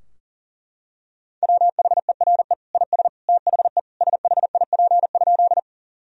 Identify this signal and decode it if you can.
Morse code, heard as a keyed tone, about 40 wpm, 700 Hz. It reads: WHERE IS THE SHIP?
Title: Sentences from Top 1000 Words - Review Part 2 - 40wpm